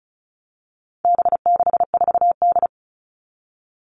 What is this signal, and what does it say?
Morse code, heard as a keyed tone, about 34 wpm, 700 Hz. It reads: B64D
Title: All Cut Numbers - 4-Digit 35wpm